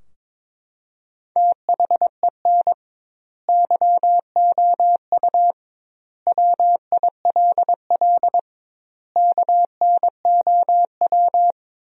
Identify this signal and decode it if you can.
Morse code, heard as a keyed tone, about 22 wpm, 700 Hz. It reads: THEN YOU WILL KNOW
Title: 2x Word-Spacing - Sentences from Top 400 Words - 22wpm